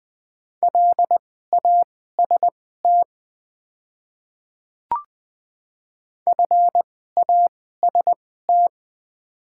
Morse code, heard as a keyed tone, about 20 wpm, spaced as for 15 wpm, 690 Hz. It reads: LAST FAST
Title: 2x Character Spacing - Top 500 Words - 20wpm